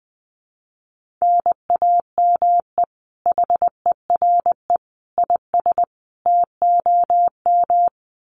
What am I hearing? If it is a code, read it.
Morse code, heard as a keyed tone, about 20 wpm, 700 Hz. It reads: NAME HERE IS TOM
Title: Common QSO Phrases 20wpm